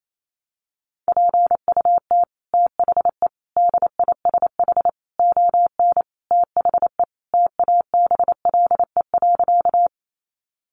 Morse code, heard as a keyed tone, about 28 wpm, 700 Hz. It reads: PUT THE DISH ON THE TABLE.
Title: Sentences from Top 1300 Words 28wpm